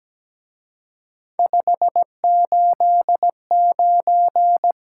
Morse code, heard as a keyed tone, about 20 wpm, 700 Hz. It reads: 589
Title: Speed-Racing Top 180 QSO Elements 17wpm